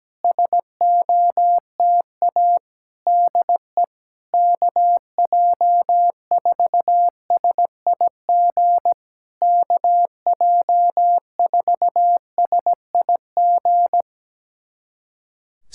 Morse code, heard as a keyed tone, about 17 wpm, 700 Hz. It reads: SOTA DE KJ4SIG KJ4SIG
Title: SOTA QSO Practice 17wpm